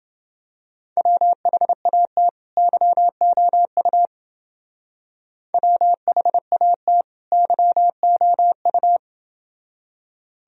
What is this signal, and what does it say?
Morse code, heard as a keyed tone, about 27 wpm, 700 Hz. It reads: WHAT YOU WHAT YOU